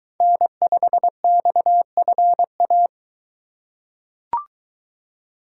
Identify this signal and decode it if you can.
Morse code, heard as a keyed tone, about 23 wpm, 700 Hz. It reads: N5XFA